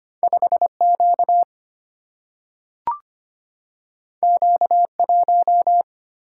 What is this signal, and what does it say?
Morse code, heard as a keyed tone, about 25 wpm, 700 Hz. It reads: 5 Q Q 1
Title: Single-Letter Single-Number 25wpm